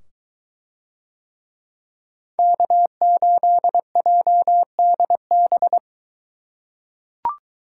Morse code, heard as a keyed tone, about 23 wpm, 700 Hz. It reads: K8JDB